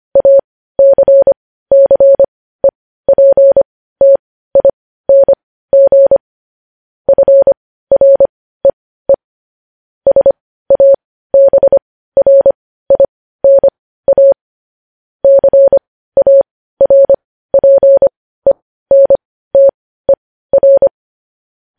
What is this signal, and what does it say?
Morse code, heard as a keyed tone, about 25 wpm, spaced as for 15 wpm, 550 Hz. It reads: ACCEPTING FREE SABRINA CARPENTER
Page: News Headlines in Morse Code at 15 WPM